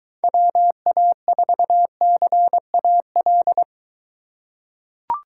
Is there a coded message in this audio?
Yes: Morse code from 0.2 to 3.6 s, heard as a keyed tone, about 23 wpm, 700 Hz. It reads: WA4CAL